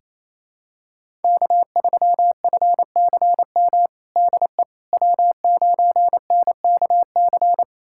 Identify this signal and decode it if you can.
Morse code, heard as a keyed tone, about 28 wpm, 700 Hz. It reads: K3FCM DE W9NKC
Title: USA Callsign DE Callsign - 5 Characters - 28wpm